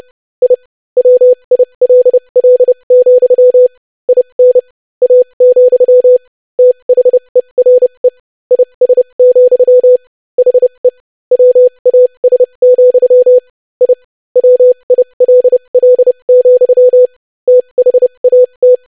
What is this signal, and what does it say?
Morse code, heard as a keyed tone, about 30 wpm, spaced as for 25 wpm, 500 Hz. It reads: I WILL, IN A, THERE IS, HE WAS, I WILL, THAT
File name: lesson / basicphrase-25wpm